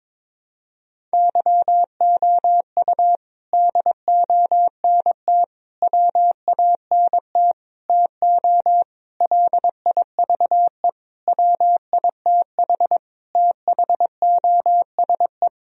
Morse code, heard as a keyed tone, about 21 wpm, 700 Hz. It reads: YOU DONT WANT TO LIVE WITH THOSE
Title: Sentences from Top 1000 Words - Review Part 1 - 22wpm